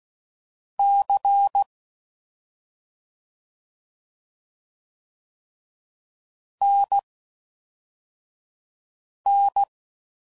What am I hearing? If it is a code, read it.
Morse code, heard as a keyed tone, about 16 wpm, spaced as for 4 wpm, 800 Hz. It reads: C NN
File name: Mixed / CW Comparisons / CW Comparisons Broken Characters